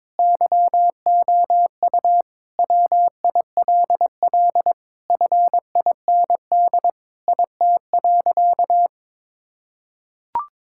Morse code, heard as a keyed tone, about 22 wpm, 700 Hz. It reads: YOU WILL FIND IT.
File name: practice / 22wpm / Sentences with 3-Word Phrases - 22wpm